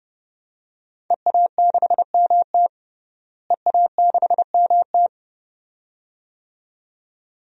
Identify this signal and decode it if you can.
Morse code, heard as a keyed tone, about 32 wpm, 700 Hz. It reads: EA6MT EA6MT
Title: DXCC-100 Call Signs - 5 Characters - Repeated 2x - 30wpm